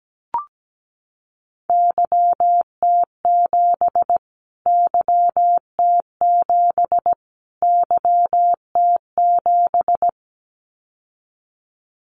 Morse code, heard as a keyed tone, about 17 wpm, 700 Hz. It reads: YT7 YT7 YT7
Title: Callsign Prefixes for Top 25 DXCC - Repeated 3x - 17wpm